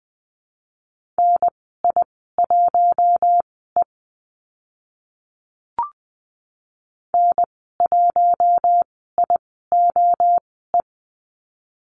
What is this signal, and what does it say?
Morse code, heard as a keyed tone, about 20 wpm, spaced as for 15 wpm, 700 Hz. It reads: NI1E N1IOE
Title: 042 Lesson 8g - Number 1 - Call Signs 20wpm